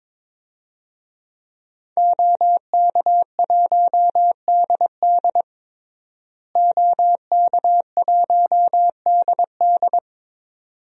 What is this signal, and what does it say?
Morse code, heard as a keyed tone, about 22 wpm, 700 Hz. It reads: OK1DD OK1DD
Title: DXCC-100 Call Signs - 5 Characters - Repeated 2x - 22wpm